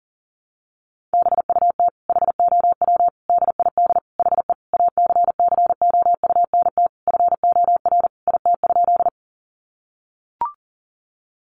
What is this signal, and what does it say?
Morse code, heard as a keyed tone, about 40 wpm, 700 Hz. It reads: BUT HOW DID HE ACCOUNT FOR IT?
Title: Sentences from Top 900 Words 40wpm